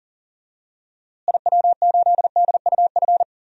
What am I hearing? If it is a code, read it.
Morse code, heard as a keyed tone, about 40 wpm, 700 Hz. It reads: IW8DUF
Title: DXCC-100 Call Signs - 6 Characters - Repeated 2x - 40wpm